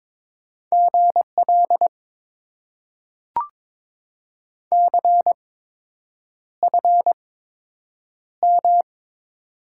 Morse code, heard as a keyed tone, about 22 wpm, spaced as for 8 wpm, 700 Hz. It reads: GL CFM
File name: Top 90 QSO Elements - 8x Character Spacing - 22wpm